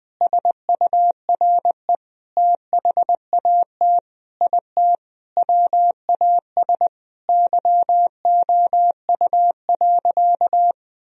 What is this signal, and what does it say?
Morse code, heard as a keyed tone, about 21 wpm, 700 Hz. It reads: SURE THAT IT WAS YOU.
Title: Sentences from Top 1000 Words - Review Part 4 - 20wpm